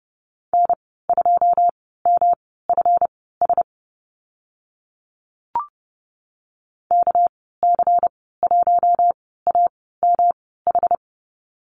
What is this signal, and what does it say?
Morse code, heard as a keyed tone, about 30 wpm, spaced as for 17 wpm, 700 Hz. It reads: N2MFS KC1AMH
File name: Lesson 37g - Letter x - Call Signs 30wpm